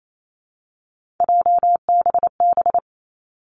Morse code, heard as a keyed tone, about 28 wpm, 700 Hz. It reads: JBB
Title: Three Challenging Letter-Number Combinations 28wpm